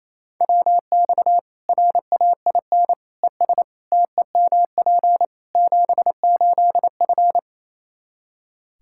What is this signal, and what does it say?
Morse code, heard as a keyed tone, about 28 wpm, 700 Hz. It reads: WX RAIN ES TEMP 78F
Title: K1USN SST QSO Practice 28wpm